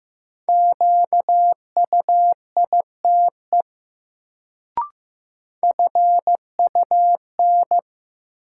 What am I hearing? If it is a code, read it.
Morse code, heard as a keyed tone, about 15 wpm, 700 Hz. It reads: QUITE FUN
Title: Sentences from Top 700 Words - New Word Drill - 15wpm